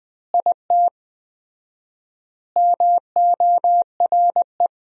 Morse code, heard as a keyed tone, about 21 wpm, 700 Hz. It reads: IT MORE
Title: 4x Word-Spacing - Sentences from Top 400 Words - 20wpm